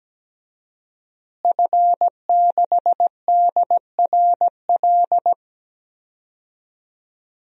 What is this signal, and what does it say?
Morse code, heard as a keyed tone, about 17 wpm, 700 Hz. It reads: F6DRL
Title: DXCC-100 Call Signs - 5 Characters - Repeated 3x - 17wpm